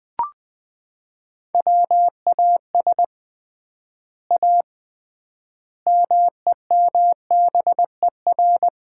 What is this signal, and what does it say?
Morse code, heard as a keyed tone, about 21 wpm, 700 Hz. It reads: WAS A MEMBER
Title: 3x Word-Spacing - Sets of 5 Words - 20wpm